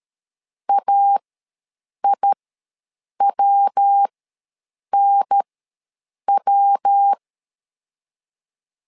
Morse code, heard as a keyed tone, about 13 wpm, spaced as for 7 wpm, 800 Hz. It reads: AIWNW
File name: LEZIONI / 3 MWRO / 3 MW ws 12wpm 1816